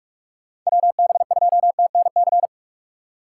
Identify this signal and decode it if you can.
Morse code, heard as a keyed tone, about 45 wpm, 700 Hz. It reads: WB2TNC
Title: USA Call Signs - 6 Characters - 6x Character Spacing - 45wpm